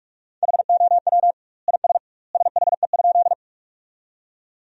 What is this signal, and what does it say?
Morse code, heard as a keyed tone, about 46 wpm, 700 Hz. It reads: HOW IS SHE?
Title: 2x Word-Spacing - Sentences from Top 100 Words - 45wpm